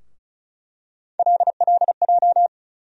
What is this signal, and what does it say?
Morse code, heard as a keyed tone, about 36 wpm, 700 Hz. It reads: LLJ
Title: USA Call Sign Postfixes - 3 Letters 35wpm